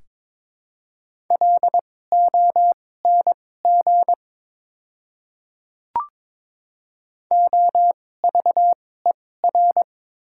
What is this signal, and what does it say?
Morse code, heard as a keyed tone, about 22 wpm, spaced as for 16 wpm, 690 Hz. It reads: LONG OVER
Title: 2x Character Spacing - Top 100 Words - 22wpm